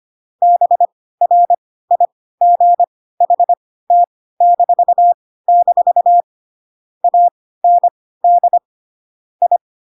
Morse code, heard as a keyed tone, about 25 wpm, spaced as for 16 wpm, 700 Hz. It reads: BRIGHT-- AND I
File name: Pinocchio / Pinocchio ch7 10 25 ik1hge